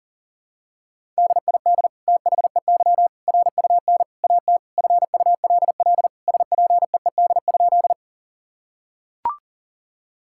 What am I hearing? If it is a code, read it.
Morse code, heard as a keyed tone, about 40 wpm, 700 Hz. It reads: DID THEY RUN AT FULL SPEED?
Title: Sentences from Top 1000 Words - Review Part 5 - 40wpm